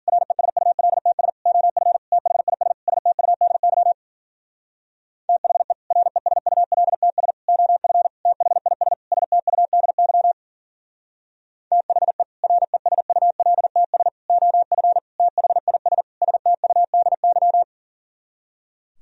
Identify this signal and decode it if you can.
Morse code, heard as a keyed tone, about 50 wpm, 700 Hz. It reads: RESULTS OF THIS STUDY THE RESULTS OF THIS STUDY THE RESULTS OF THIS STUDY
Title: Speed-Racing - Sets of 5 Words 40wpm